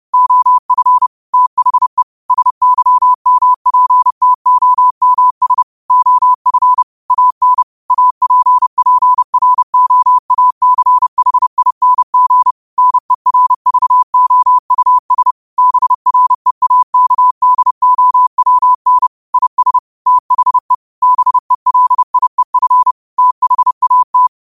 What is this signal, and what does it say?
Morse code, heard as a keyed tone, about 30 wpm, 1000 Hz. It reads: OF THE SYMPTOMS OF AN APPROACHING NERVOUS BREAKDOWN IS THE BELIEF THAT
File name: qotdfiles / 30wpm